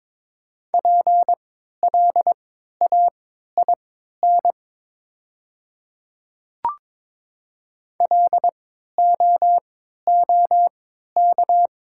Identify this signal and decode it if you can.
Morse code, heard as a keyed tone, about 22 wpm, spaced as for 12 wpm, 690 Hz. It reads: PLAIN LOOK
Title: 3x Character Spacing - Top 500 Words - 22wpm